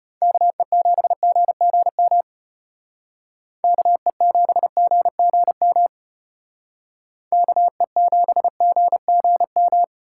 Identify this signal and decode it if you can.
Morse code, heard as a keyed tone, about 34 wpm, 700 Hz. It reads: KE7GGM KE7GGM KE7GGM